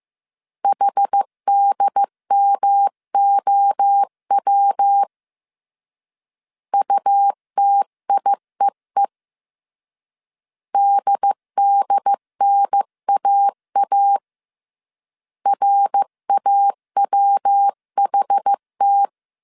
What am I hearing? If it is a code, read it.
Morse code, heard as a keyed tone, about 15 wpm, 800 Hz. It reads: HDMOW UTIEE DDNAA RAWHT
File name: CW / DET / 1 3 1500C S C 14wpm 2458